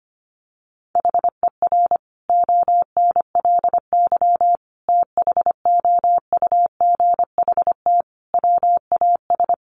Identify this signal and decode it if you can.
Morse code, heard as a keyed tone, about 25 wpm, 700 Hz. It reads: HER ONLY THOUGHT WAS
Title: Sentences from Top 400 Words 25wpm